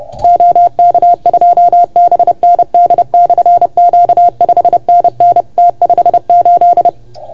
Morse code, heard as a keyed tone, about 30 wpm, 690 Hz. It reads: OK2BND/Q 5NN T58
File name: Archiv / CUC381 / OK1FGD CUC 381